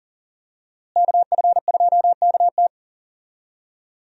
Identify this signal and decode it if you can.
Morse code, heard as a keyed tone, about 39 wpm, 700 Hz. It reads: KF2KT
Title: USA Call Signs - 5 Characters - Repeated 2x - 40wpm